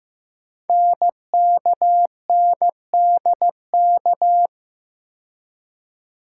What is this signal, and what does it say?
Morse code, heard as a keyed tone, about 15 wpm, 700 Hz. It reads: NKNDK
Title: Code Groups - 5 Letters - 15wpm